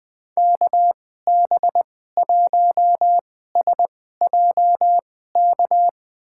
Morse code, heard as a keyed tone, about 21 wpm, spaced as for 15 wpm, 700 Hz. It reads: KB1SJK
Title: USA Call Signs - 6 Characters - 2x Character Spacing - 20wpm